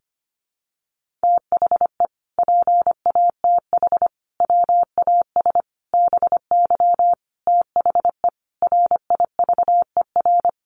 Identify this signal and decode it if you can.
Morse code, heard as a keyed tone, about 25 wpm, 700 Hz. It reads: THE PATH WAS BY THE RIVER
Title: Sentences from Top 1400 Words 25wpm